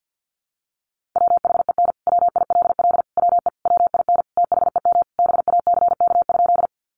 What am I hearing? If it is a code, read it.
Morse code, heard as a keyed tone, about 50 wpm, 700 Hz. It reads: WHEN WILL WE WIN THEM BACK?
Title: Sentences from Top 1500 Words 50wpm